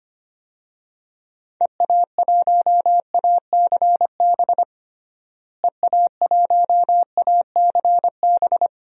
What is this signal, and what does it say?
Morse code, heard as a keyed tone, about 25 wpm, 700 Hz. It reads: EA1ACB EA1ACB
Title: DXCC-100 Call Signs - 6 Characters - Repeated 3x - 25wpm